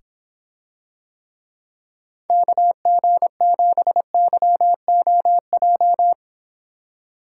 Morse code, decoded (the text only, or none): KG7YOJ